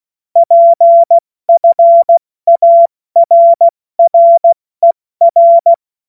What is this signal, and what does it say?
Morse code, heard as a keyed tone, about 16 wpm, 680 Hz. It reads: PFARRER
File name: Morsekurs / Witze / Witz 09. Dezember 2025 70 F80